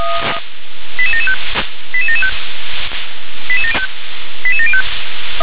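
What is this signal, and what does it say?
Five-tone selective call (ZVEI), heard as quick runs of five tones, tones of about 70 ms. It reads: 88775 88775 88705
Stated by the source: ZVEI five-tone call received by SDR.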